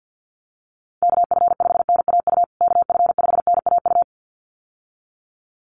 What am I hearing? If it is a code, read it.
Morse code, heard as a keyed tone, about 50 wpm, 700 Hz. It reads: KF5NAU KF5NAU
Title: USA Call Signs Repeated 2x 50wpm